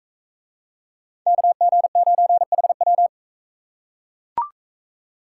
Morse code, heard as a keyed tone, about 42 wpm, 700 Hz. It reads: KG9HW